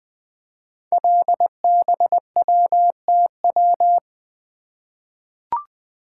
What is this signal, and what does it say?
Morse code, heard as a keyed tone, about 20 wpm, 700 Hz. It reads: LBWTW
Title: Code Groups - 5 Letter:Numbers - 20wpm